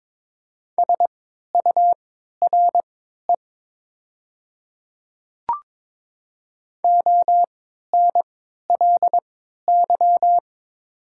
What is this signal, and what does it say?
Morse code, heard as a keyed tone, about 22 wpm, spaced as for 13 wpm, 700 Hz. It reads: SURE ONLY
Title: 3x Character Spacing - Top 300 Words - 22wpm